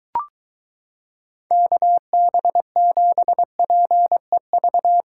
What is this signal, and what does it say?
Morse code, heard as a keyed tone, about 22 wpm, 700 Hz. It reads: KB7PEV